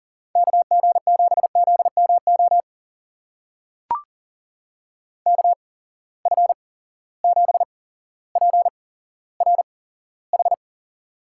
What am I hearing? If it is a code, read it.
Morse code, heard as a keyed tone, about 37 wpm, spaced as for 17 wpm, 700 Hz. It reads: KG7ZMO KF7PRH
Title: USA Call Signs - 6 Characters - 8x Character Spacing - 40wpm